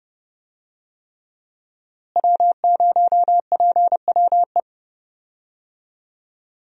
Morse code, heard as a keyed tone, about 32 wpm, 700 Hz. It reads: W0PWE
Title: USA Call Signs - 5 Characters - Repeated 3x - 30wpm